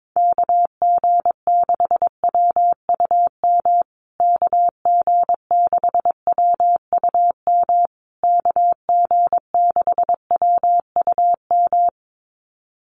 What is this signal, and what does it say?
Morse code, heard as a keyed tone, about 22 wpm, 690 Hz. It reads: KG6WUM KG6WUM KG6WUM
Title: USA Call Signs Repeated 3x 22wpm